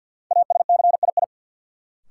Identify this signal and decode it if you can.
Morse code, heard as a keyed tone, about 53 wpm, 700 Hz. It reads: ASCII
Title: Sets of 5 Words - New Word Drill - 50wpm